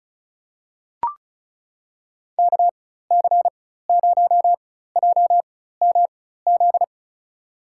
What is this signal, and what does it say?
Morse code, heard as a keyed tone, about 34 wpm, spaced as for 16 wpm, 700 Hz. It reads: KC0JMZ
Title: USA Call Signs - 6 Characters - 4x Character Spacing - 35wpm